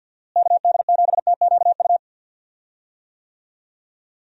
Morse code, heard as a keyed tone, about 53 wpm, 700 Hz. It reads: KD7TQU
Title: USA Call Signs - 6 Characters - Repeated 3x - 50wpm